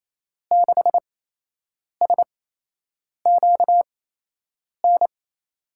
Morse code, heard as a keyed tone, about 28 wpm, spaced as for 8 wpm, 700 Hz. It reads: 6SQN
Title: USA Call Signs - 6 Characters - 8x Character Spacing - 28wpm